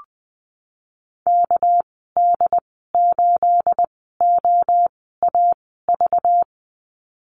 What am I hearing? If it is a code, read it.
Morse code, heard as a keyed tone, about 20 wpm, spaced as for 15 wpm, 700 Hz. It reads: KD8OAV
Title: 238 Lesson 38g - Letter q - Call Signs 20wpm